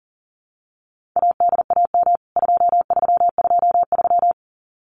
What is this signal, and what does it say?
Morse code, heard as a keyed tone, about 40 wpm, 700 Hz. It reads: ADAM 2323
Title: CWT Exchange 40wpm